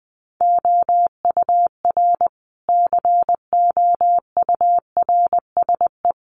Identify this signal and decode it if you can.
Morse code, heard as a keyed tone, about 20 wpm, 700 Hz. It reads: OUR COURSE
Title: Sentences from Top 1300 Words 20wpm